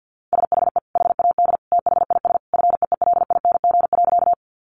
Morse code, heard as a keyed tone, about 49 wpm, 700 Hz. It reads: SHE HAD THIS FEELING.
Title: Sentences from Top 1200 Words 50wpm